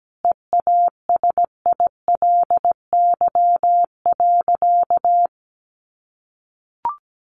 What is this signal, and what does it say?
Morse code, heard as a keyed tone, about 17 wpm, 700 Hz. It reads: EASILY.
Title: Sentences from Top 900 Words 17wpm